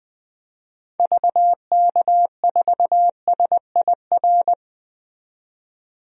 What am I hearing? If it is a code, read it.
Morse code, heard as a keyed tone, about 20 wpm, 700 Hz. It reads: VK4SIR
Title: DXCC-100 Call Signs - 6 Characters - 20wpm